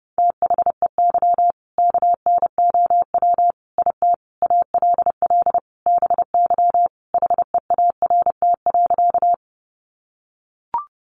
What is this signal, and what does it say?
Morse code, heard as a keyed tone, about 30 wpm, 700 Hz. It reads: THEY KNOW IT ALL BY HEART.
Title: Sentences from Top 1200 Words 30wpm